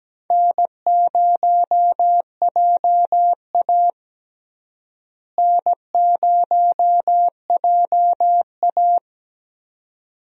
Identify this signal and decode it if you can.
Morse code, heard as a keyed tone, about 17 wpm, 700 Hz. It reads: N0JA N0JA
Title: USA Call Signs - 4 Characters - Repeated 2x - 17wpm